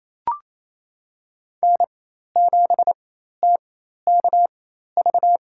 Morse code, heard as a keyed tone, about 28 wpm, spaced as for 13 wpm, 700 Hz. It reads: N7TKV
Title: USA Call Signs - 4x Character Spacing - 28wpm